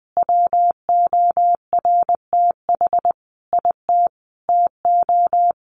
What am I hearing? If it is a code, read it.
Morse code, heard as a keyed tone, about 20 wpm, 700 Hz. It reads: WORTH IT TO
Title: Sentences from Top 800 Words 20wpm